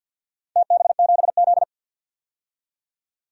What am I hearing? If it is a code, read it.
Morse code, heard as a keyed tone, about 46 wpm, 700 Hz. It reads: TB7Z
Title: DXCC-100 Call Signs - 4 Characters - 50wpm